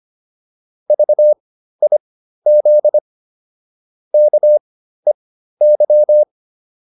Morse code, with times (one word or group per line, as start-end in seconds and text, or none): 0.9-3.0 VIZ
4.1-6.2 KEY